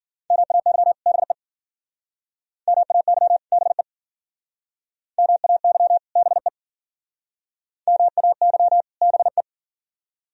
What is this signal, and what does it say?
Morse code, heard as a keyed tone, about 50 wpm, 700 Hz. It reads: MAY BE MAY BE MAY BE MAY BE
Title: Speed-Racing - Sets of 2 Words 40wpm